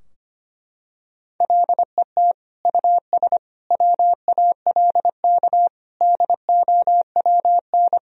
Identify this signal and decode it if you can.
Morse code, heard as a keyed tone, about 25 wpm, 700 Hz. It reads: LET US WALK DOWN
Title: Sentences from Top 1000 Words - Review Part 3 - 25wpm